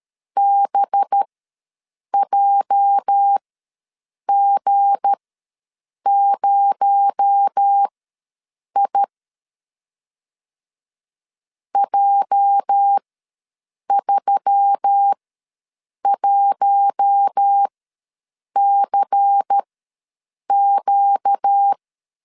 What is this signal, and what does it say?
Morse code, heard as a keyed tone, about 13 wpm, spaced as for 7 wpm, 800 Hz. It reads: BJG0I J31CQ
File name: LEZIONI / 8 YJ45 / 8 YJ ws 12wpm 3342